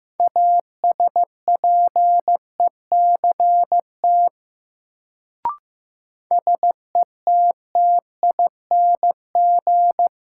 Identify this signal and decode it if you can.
Morse code, heard as a keyed tone, about 15 wpm, 700 Hz. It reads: ASPECT SETTING
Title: Sentences from Top 1300 Words - New Word Drill - 15wpm